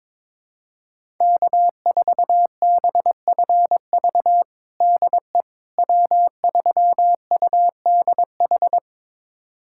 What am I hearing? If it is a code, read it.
Morse code, heard as a keyed tone, about 22 wpm, 700 Hz. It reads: K4BFV DE W3UDH
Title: USA Callsign DE Callsign - 5 Characters - 22wpm